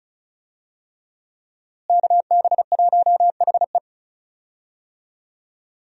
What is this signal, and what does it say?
Morse code, heard as a keyed tone, about 36 wpm, 700 Hz. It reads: KB1HE
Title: USA Call Signs - 5 Characters - Repeated 3x - 35wpm